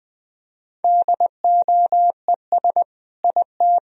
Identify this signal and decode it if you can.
Morse code, heard as a keyed tone, about 21 wpm, 700 Hz. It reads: DOES IT
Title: Sentences from Top 1000 Words - Review Part 2 - 20wpm